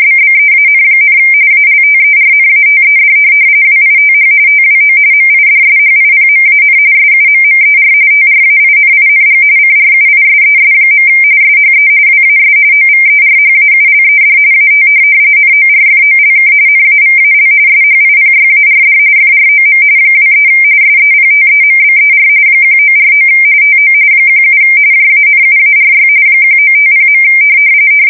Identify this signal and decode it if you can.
RTTY, about 45.45 baud. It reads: BBEAN, ALTERNATED WITH TWO MIDSHIPMAN SUMMER TRAINING CRUISES. SHE WAS OVERHAULED AT NORFOLK NAVAL SHIPYARD FROM 23 SEPTEMBER 1949 TO 17 JANUARY 1950.
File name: fsk170 / USS-Missouri